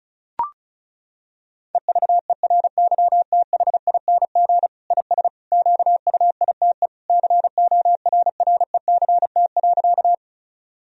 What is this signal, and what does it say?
Morse code, heard as a keyed tone, about 36 wpm, 700 Hz. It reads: EVERYTHING IS QUITE CORRECT.
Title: Sentences from Top 1000 Words - Review Part 2 - 35wpm